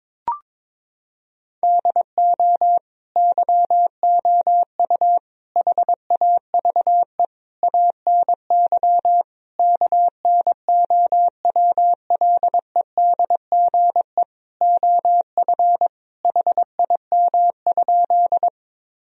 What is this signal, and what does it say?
Morse code, heard as a keyed tone, about 22 wpm, 700 Hz. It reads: DO YOU HAVE ANY KNOWLEDGE OF HIM?
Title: Sentences from Top 1000 Words - Review Part 3 - 22wpm